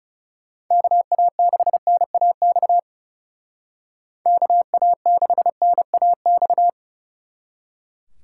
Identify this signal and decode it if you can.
Morse code, heard as a keyed tone, about 39 wpm, 700 Hz. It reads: KA6NAX KA6NAX